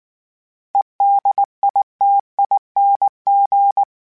Morse code, heard as a keyed tone, about 19 wpm, 800 Hz. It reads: EDITING
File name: Special INGs / Special INGs-mp3s / Sngl Ing @18